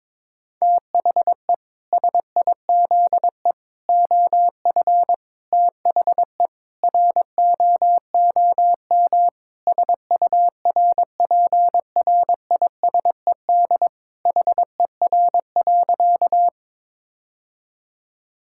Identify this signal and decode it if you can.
Morse code, heard as a keyed tone, about 22 wpm, 700 Hz. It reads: THE SIZE OF THE ROOM SURPRISED HER.
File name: practice / 22wpm / Sentences with 4-Word Phrases - 22wpm